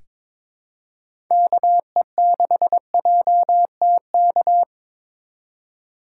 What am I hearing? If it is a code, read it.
Morse code, heard as a keyed tone, about 22 wpm, 700 Hz. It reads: KE6JTK
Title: USA Call Signs - 6 Characters - 2x Character Spacing - 22wpm